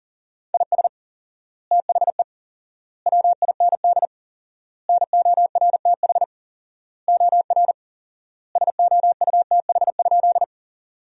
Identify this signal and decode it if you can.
Morse code, heard as a keyed tone, about 37 wpm, 700 Hz. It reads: IS THE WIND NORTH OR SOUTH?
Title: 4x Word-Spacing - Sentences from Top 400 Words - 40wpm